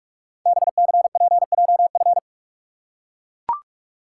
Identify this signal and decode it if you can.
Morse code, heard as a keyed tone, about 44 wpm, 700 Hz. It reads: BCPJF
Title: Code Groups - 5 Letters - 45wpm